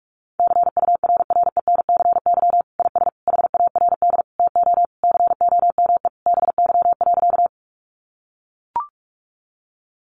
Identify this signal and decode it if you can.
Morse code, heard as a keyed tone, about 44 wpm, 700 Hz. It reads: CURRENCY IS HARD TO COME BY.
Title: Sentences from Top 1500 Words 45wpm